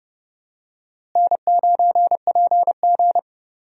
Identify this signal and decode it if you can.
Morse code, heard as a keyed tone, about 31 wpm, 700 Hz. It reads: N9PG
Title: USA Call Signs - 4 Characters - Repeated 2x - 30wpm